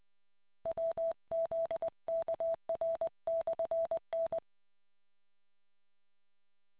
Morse code, heard as a keyed tone, about 20 wpm, 660 Hz. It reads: WZKR/N